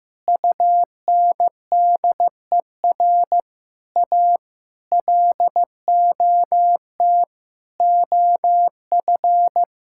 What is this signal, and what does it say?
Morse code, heard as a keyed tone, about 15 wpm, 700 Hz. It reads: UNDER A LOT OF
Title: Sentences from Top 1000 Words - Review Part 6 - 15wpm